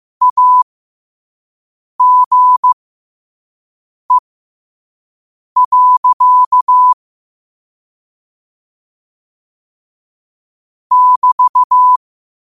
Morse code, heard as a keyed tone, about 15 wpm, spaced as for 5 wpm, 1000 Hz. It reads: AGE. <BT>
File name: qotdfiles / 5wpm